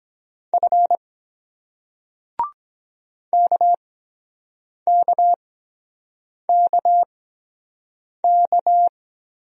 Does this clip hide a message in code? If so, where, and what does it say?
Yes: Morse code from 0.5 to 8.9 s, heard as a keyed tone, about 23 wpm, 700 Hz. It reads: F K K K K